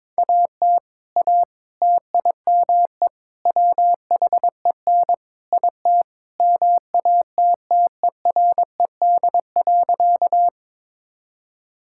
Morse code, heard as a keyed tone, about 22 wpm, 700 Hz. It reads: AT A TIME WHEN IT MATTERED.
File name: practice / 22wpm / Sentences with 4-Word Phrases - 22wpm